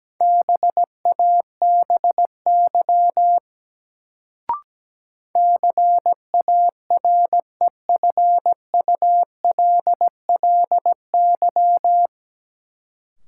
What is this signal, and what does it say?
Morse code, heard as a keyed tone, about 17 wpm, 700 Hz. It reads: BABY CAREFULLY
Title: Sentences from Top 1300 Words - New Word Drill - 17wpm